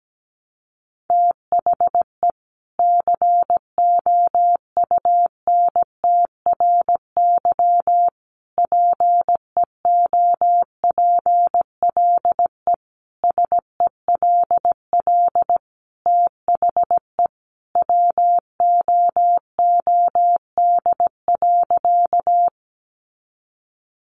Morse code, heard as a keyed tone, about 17 wpm, 700 Hz. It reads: THE COUNTRY PEOPLE SELL THE WOOD.
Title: Sentences from Top 1100 Words 17wpm